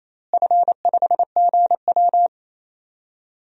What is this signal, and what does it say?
Morse code, heard as a keyed tone, about 28 wpm, 700 Hz. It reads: F5GW
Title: DXCC-100 Call Signs - 4 Characters - Repeated 2x - 28wpm